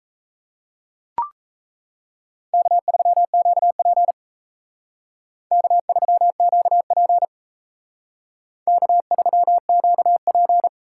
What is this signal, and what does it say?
Morse code, heard as a keyed tone, about 37 wpm, 700 Hz. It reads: K3QP K3QP K3QP